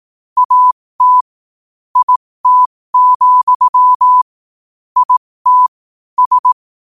Morse code, heard as a keyed tone, about 18 wpm, spaced as for 15 wpm, 1000 Hz. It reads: AT IT, IT S
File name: qotdfiles / 15wpm